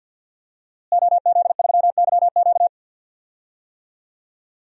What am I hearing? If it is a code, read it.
Morse code, heard as a keyed tone, about 52 wpm, 700 Hz. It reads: OZ3YQ
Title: DXCC-100 Call Signs - 5 Characters - Repeated 2x - 50wpm